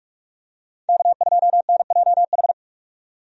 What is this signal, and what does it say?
Morse code, heard as a keyed tone, about 46 wpm, 700 Hz. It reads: K2NJH
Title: USA Call Signs - 5 Characters - 45wpm